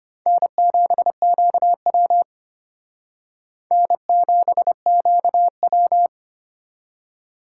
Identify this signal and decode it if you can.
Morse code, heard as a keyed tone, about 32 wpm, 700 Hz. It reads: N7QW N7QW